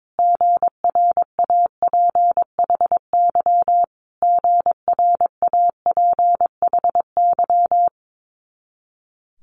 Morse code, heard as a keyed tone, about 22 wpm, 700 Hz. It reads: GRAPHY GRAPHY
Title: Multi-letter Phonemes Repeated 3x 22wpm